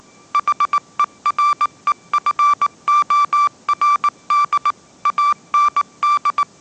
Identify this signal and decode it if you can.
Morse code, heard as a keyed tone, about 19 wpm, 1170 Hz. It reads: HEREFORD AND
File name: 20wpm